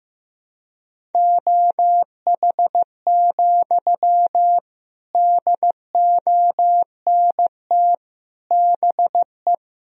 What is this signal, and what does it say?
Morse code, heard as a keyed tone, about 15 wpm, 700 Hz. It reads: OH, DONT BE
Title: Sentences from Top 1000 Words - Review Part 1 - 15wpm